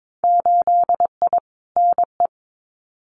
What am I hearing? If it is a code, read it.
Morse code, heard as a keyed tone, about 22 wpm, 700 Hz. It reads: 8I NE
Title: Winter Field Day Exchanges Encore 22wpm